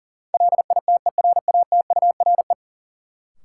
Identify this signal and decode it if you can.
Morse code, heard as a keyed tone, about 40 wpm, 700 Hz. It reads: LITERATURE
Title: Sentences from Top 600 Words - New Word Drill - 40wpm